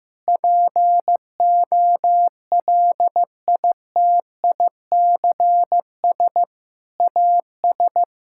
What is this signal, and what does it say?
Morse code, heard as a keyed tone, about 15 wpm, 700 Hz. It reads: POLITICS AS
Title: Sentences from Top 1000 Words - Review Part 6 - 15wpm